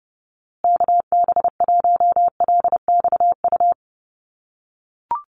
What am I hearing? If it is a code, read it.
Morse code, heard as a keyed tone, about 30 wpm, 700 Hz. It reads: KB1LXU
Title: USA Call Signs Repeated 3x 30wpm